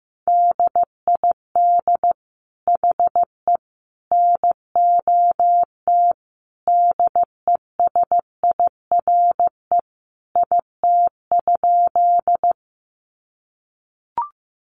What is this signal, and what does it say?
Morse code, heard as a keyed tone, about 15 wpm, 700 Hz. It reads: DID HE NOT DESIRE IT?
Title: Sentences from Top 1100 Words 15wpm